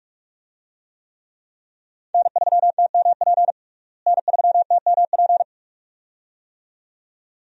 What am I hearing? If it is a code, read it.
Morse code, heard as a keyed tone, about 46 wpm, 700 Hz. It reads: N3TMP N3TMP
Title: USA Call Signs - 5 Characters - Repeated 2x - 45wpm